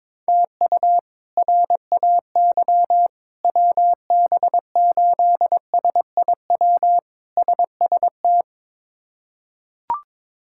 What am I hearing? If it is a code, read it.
Morse code, heard as a keyed tone, about 22 wpm, 700 Hz. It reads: TU RAY WB8SIW SST
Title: K1USN SST QSO Practice - 22wpm